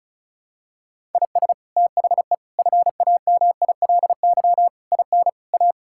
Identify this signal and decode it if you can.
Morse code, heard as a keyed tone, about 34 wpm, 700 Hz. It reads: IS THE FAMILY IN A
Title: Sentences from Top 1000 Words - Review Part 5 - 35wpm